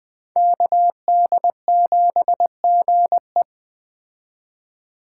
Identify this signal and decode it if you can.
Morse code, heard as a keyed tone, about 20 wpm, 700 Hz. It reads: KD7GE
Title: Speed Racing - USA Call Signs 17wpm